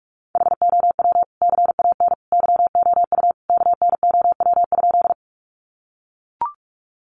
Morse code, heard as a keyed tone, about 44 wpm, 700 Hz. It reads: HOW CAN YOU KNOW?
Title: Sentences from Top 100 Words 45wpm